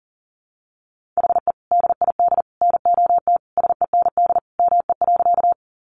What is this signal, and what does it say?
Morse code, heard as a keyed tone, about 40 wpm, 700 Hz. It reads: HE DID NOT SEND ME.
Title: Sentences from Top 1500 Words 40wpm